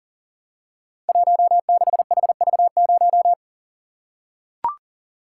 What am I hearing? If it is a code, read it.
Morse code, heard as a keyed tone, about 42 wpm, 700 Hz. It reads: 16HV0
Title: Code Groups - 5 Letter:Numbers - 40wpm